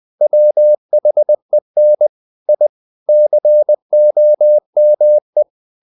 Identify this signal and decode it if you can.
Morse code, heard as a keyed tone, about 20 wpm, 600 Hz. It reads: WHEN I COME
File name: the old man and the sea / the old man and the sea part 1 20wpm